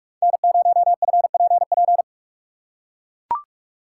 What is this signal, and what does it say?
Morse code, heard as a keyed tone, about 43 wpm, 700 Hz. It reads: N0FPP
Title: USA Call Signs - 5 Characters - 6x Character Spacing - 45wpm